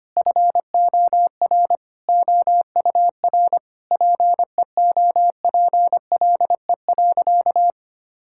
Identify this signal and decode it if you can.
Morse code, heard as a keyed tone, about 25 wpm, 700 Hz. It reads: FOR OUR PEOPLE.
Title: Sentences from Top 1000 Words - Review Part 5 - 25wpm